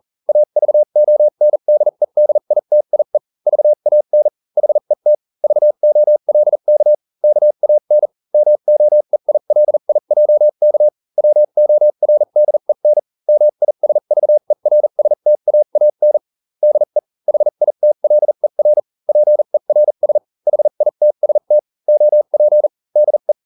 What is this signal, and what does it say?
Morse code, heard as a keyed tone, about 40 wpm, spaced as for 35 wpm, 600 Hz. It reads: AVONDEDITIE VAN HET VOLK KAN MOEILIJK WORDEN MISVERSTAAN DE HITLER PERS HITST OP DE